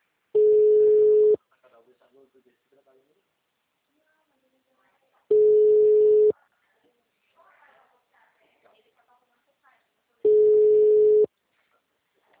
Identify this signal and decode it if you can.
Ringback tone (Europe, 425 Hz), with speech.
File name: out- -2019- - - .67958